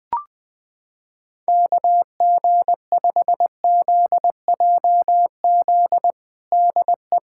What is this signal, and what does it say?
Morse code, heard as a keyed tone, about 20 wpm, 700 Hz. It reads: KG5ZJZ DE
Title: USA Callsign DE Callsign - 6 Characters - 20wpm